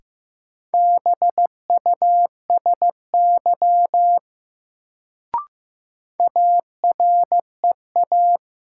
Morse code, heard as a keyed tone, about 15 wpm, 700 Hz. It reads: BUSY AREA
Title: Sentences from Top 1000 Words - Review Part 2 - New Word Drill - 15wpm